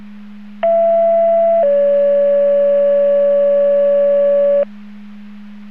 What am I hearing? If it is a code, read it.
Two-tone page, heard as a one-second tone, then a three-second tone, about 670 then 570 Hz. It is heard from 0.6 to 4.6 s.